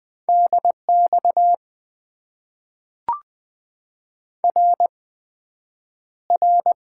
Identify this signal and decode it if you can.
Morse code, heard as a keyed tone, about 20 wpm, 700 Hz. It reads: DX R R
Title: Top 90 QSO Elements - 8x Character Spacing - 20wpm